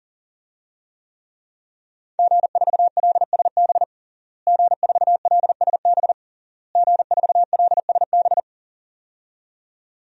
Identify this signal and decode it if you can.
Morse code, heard as a keyed tone, about 40 wpm, 700 Hz. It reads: G4LSB G4LSB G4LSB
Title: DXCC-100 Call Signs - 5 Characters - Repeated 3x - 40wpm